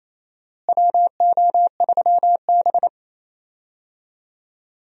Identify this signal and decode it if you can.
Morse code, heard as a keyed tone, about 28 wpm, 700 Hz. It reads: WO3B
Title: USA Call Signs - 4 Characters - Repeated 3x - 28wpm